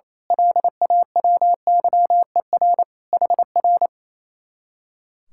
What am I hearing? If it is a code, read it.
Morse code, heard as a keyed tone, about 28 wpm, 700 Hz. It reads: LAWYER HR
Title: Speed-Racing Common QSO Phrases - 28wpm